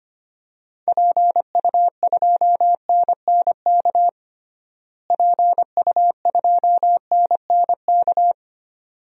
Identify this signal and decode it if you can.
Morse code, heard as a keyed tone, about 25 wpm, 700 Hz. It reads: PU2NNK PU2NNK
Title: DXCC-100 Call Signs - 6 Characters - Repeated 2x - 25wpm